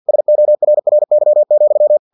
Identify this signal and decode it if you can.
Morse code, heard as a keyed tone, about 50 wpm, 600 Hz. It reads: SORRY,
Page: Never 49wpm